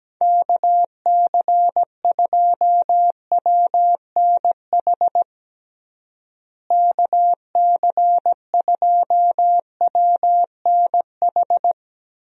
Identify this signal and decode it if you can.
Morse code, heard as a keyed tone, about 17 wpm, 700 Hz. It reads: KC2WNH KC2WNH
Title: USA Call Signs - 6 Characters - Repeated 3x - 17wpm